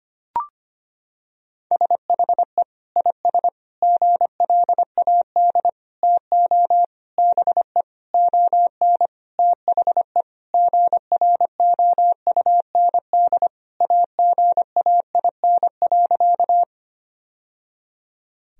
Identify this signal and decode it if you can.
Morse code, heard as a keyed tone, about 25 wpm, 700 Hz. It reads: SHE IS GLAD TO BE ON THE GROUND AGAIN.
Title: Sentences from Top 1000 Words - Review Part 2 - 25wpm